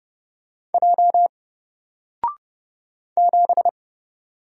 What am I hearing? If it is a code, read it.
Morse code, heard as a keyed tone, about 30 wpm, 700 Hz. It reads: J 7
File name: Single-Letter-Number-30wpm